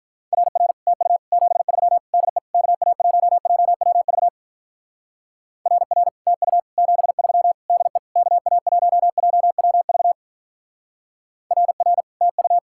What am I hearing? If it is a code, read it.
Morse code, heard as a keyed tone, about 46 wpm, 700 Hz. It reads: RR TU 73 DE KA1JWV RR TU 73 DE KA1JWV RR TU